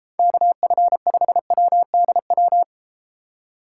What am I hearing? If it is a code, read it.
Morse code, heard as a keyed tone, about 32 wpm, 700 Hz. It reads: KF5WDW